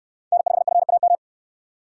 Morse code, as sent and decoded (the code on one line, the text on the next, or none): -. ..... ..-. .- --
N5FAM